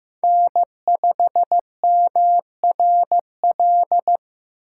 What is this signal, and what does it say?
Morse code, heard as a keyed tone, about 15 wpm, 700 Hz. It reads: N5MRL
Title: USA Call Signs - 5 Characters - 3x Character Spacing - 15wpm